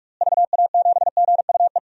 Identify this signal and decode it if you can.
Morse code, heard as a keyed tone, about 41 wpm, 700 Hz. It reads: VA7GUE
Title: DXCC-100 Call Signs - 6 Characters - Repeated 3x - 45wpm